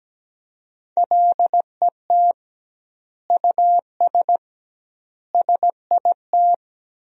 Morse code, heard as a keyed tone, about 17 wpm, 700 Hz. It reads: LET US SIT
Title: 2x Word-Spacing - Sentences from Top 500 Words - 17wpm